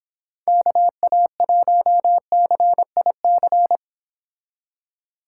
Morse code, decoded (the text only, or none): KA1CIC